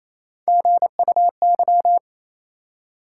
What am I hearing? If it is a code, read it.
Morse code, heard as a keyed tone, about 28 wpm, 700 Hz. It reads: GUY
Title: Three Letter Words - 28wpm